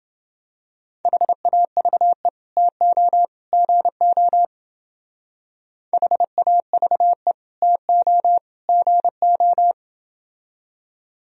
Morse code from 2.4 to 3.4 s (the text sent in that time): TO